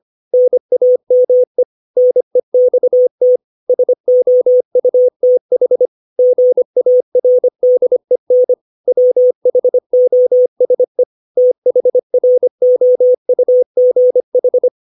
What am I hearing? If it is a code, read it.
Morse code, heard as a keyed tone, about 25 wpm, 500 Hz. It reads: NAME NEXT SOUTH GARDEN WHOSE THROUGH
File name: 632 words / 632 Words 25WPM